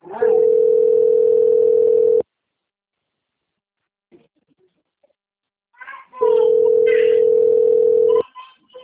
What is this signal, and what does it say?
Ringback tone (North America), 440 + 480 Hz.